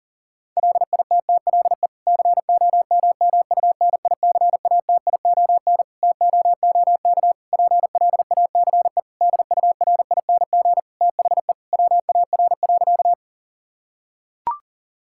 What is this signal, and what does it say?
Morse code, heard as a keyed tone, about 40 wpm, 700 Hz. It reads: LITTLE COMMUNICATION TOOK PLACE DURING THE WAR.
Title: Sentences from Top 1000 Words - Review Part 6 - 40wpm